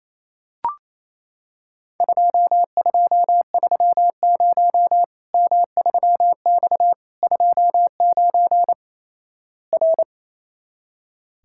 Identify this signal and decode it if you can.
Morse code, heard as a keyed tone, about 27 wpm, 700 Hz. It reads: 2230 M3X 29 R
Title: Worked All Europe DX Contest - QTC Exchange - 28wpm